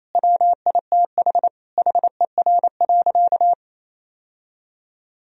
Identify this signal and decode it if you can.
Morse code, heard as a keyed tone, about 28 wpm, 700 Hz. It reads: WITH HER.
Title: Sentences from Top 1000 Words - Review Part 4 - 28wpm